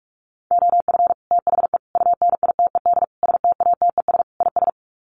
Morse code, heard as a keyed tone, about 44 wpm, 690 Hz. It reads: OF THE UNITED STATES IS
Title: Sets of 5 Words Encore2 45wpm